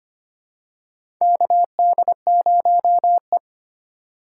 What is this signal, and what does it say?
Morse code, heard as a keyed tone, about 25 wpm, 700 Hz. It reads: KD0E
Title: USA Call Signs - 4 Characters - 3x Character Spacing - 25wpm